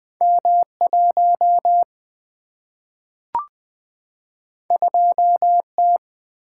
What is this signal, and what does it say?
Morse code, heard as a keyed tone, about 20 wpm, 700 Hz. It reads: M 1 2 T
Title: Single-Letter Single-Number 20wpm